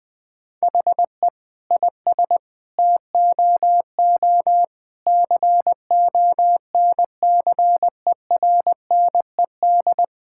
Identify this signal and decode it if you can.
Morse code, heard as a keyed tone, about 20 wpm, 700 Hz. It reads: HE IS TOO CONCERNED
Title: Sentences from Top 1000 Words - Review Part 6 - 20wpm